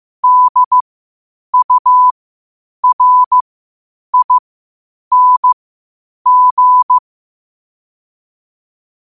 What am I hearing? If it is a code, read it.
Morse code, heard as a keyed tone, about 15 wpm, spaced as for 8 wpm, 1000 Hz. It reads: DURING
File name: qotdfiles / 8wpm